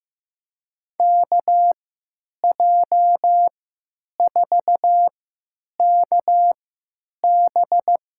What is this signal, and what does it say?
Morse code, heard as a keyed tone, about 15 wpm, spaced as for 9 wpm, 700 Hz. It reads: KJ4KB
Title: USA Call Signs - 3x Character Spacing - 15wpm